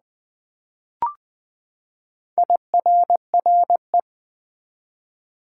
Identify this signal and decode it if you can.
Morse code, heard as a keyed tone, about 20 wpm, 700 Hz. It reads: IRRE
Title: Build A Word - 4 Letters At A Time - 10-Letter Words - 20wpm